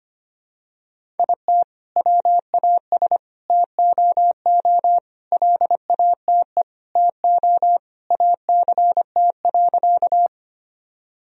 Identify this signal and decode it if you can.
Morse code, heard as a keyed tone, about 25 wpm, 700 Hz. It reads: IT WAS TOO LATE TO ACT.
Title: Sentences from Top 1000 Words - Review Part 1 - 25wpm